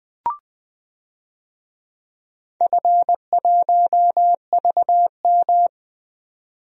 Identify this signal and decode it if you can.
Morse code, heard as a keyed tone, about 20 wpm, 700 Hz. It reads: F1VM
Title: DXCC-100 Call Signs - 4 Characters - 20wpm